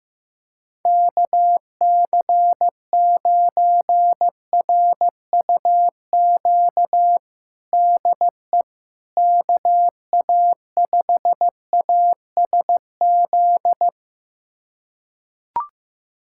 Morse code, heard as a keyed tone, about 15 wpm, 700 Hz. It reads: KC9RUQ DE KA5ASZ
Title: USA Callsign DE Callsign - 6 Characters - 15wpm